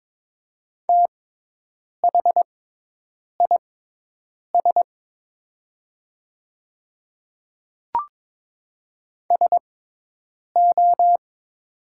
Morse code, heard as a keyed tone, about 22 wpm, spaced as for 8 wpm, 700 Hz. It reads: THIS SO
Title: 6x Character Spacing - Top 100 Words - 22wpm